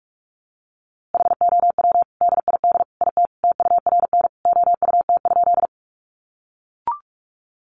Morse code, heard as a keyed tone, about 44 wpm, 700 Hz. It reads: HOW DID IT TURN OUT?
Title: A Sound Of Thunder 45wpm